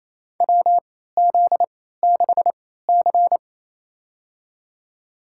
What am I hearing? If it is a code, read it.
Morse code, heard as a keyed tone, about 27 wpm, spaced as for 16 wpm, 700 Hz. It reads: WZ6C
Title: USA Call Signs - 3x Character Spacing - 28wpm